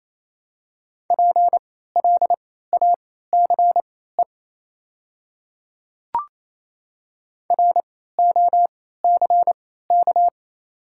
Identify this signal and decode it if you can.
Morse code, heard as a keyed tone, about 28 wpm, spaced as for 16 wpm, 700 Hz. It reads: PLACE ROCK
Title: 3x Character Spacing - Top 500 Words - 28wpm